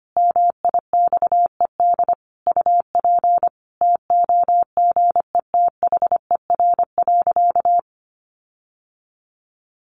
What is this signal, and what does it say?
Morse code, heard as a keyed tone, about 25 wpm, 700 Hz. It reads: MIXED UP TOGETHER.
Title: Sentences from Top 1500 Words 25wpm